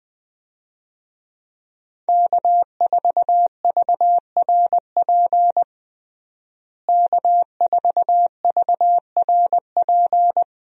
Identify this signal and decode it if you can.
Morse code, heard as a keyed tone, about 20 wpm, 700 Hz. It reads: K4VRP K4VRP
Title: USA Call Signs - 5 Characters - Repeated 2x - 20wpm